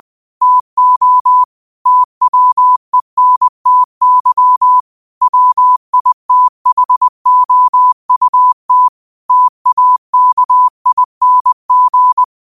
Morse code, heard as a keyed tone, about 20 wpm, 1000 Hz. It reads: TO TWENTY WITHOUT TAKING
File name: qotdfiles / 20wpm